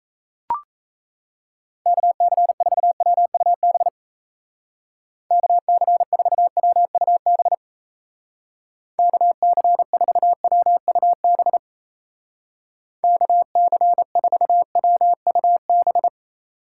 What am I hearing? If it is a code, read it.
Morse code, heard as a keyed tone, about 36 wpm, 700 Hz. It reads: KC4WUB KC4WUB KC4WUB KC4WUB